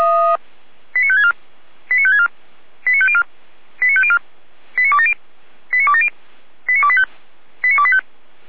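Five-tone selective call (ZVEI), heard as quick runs of five tones, tones of about 70 ms. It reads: 89564 89564 89504 89504 89270 89270 89285 89285
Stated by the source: ZVEI five-tone selective call received on SDR.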